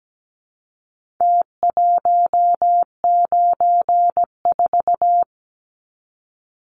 Morse code, heard as a keyed tone, about 17 wpm, 700 Hz. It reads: T194
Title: Numbers - Contest Style - 4-Digit 17wpm